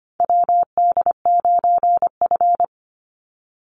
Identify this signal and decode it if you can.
Morse code, heard as a keyed tone, about 25 wpm, 700 Hz. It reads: WD9F
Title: NA SOTA Chaser Callsigns 25wpm